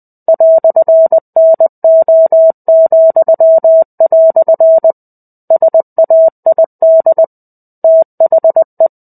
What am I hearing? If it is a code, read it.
Morse code, heard as a keyed tone, about 20 wpm, 650 Hz. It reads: "NO," SAID THE
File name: 05-Chapter-05 20 WPM